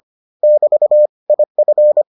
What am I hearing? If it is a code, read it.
Morse code, heard as a keyed tone, about 25 wpm, 600 Hz. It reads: <BT> IF